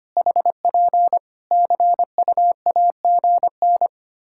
Morse code, heard as a keyed tone, about 25 wpm, 700 Hz. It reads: HP CUAGN
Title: Practice QSO - Exchange 4 - Ending - 25wpm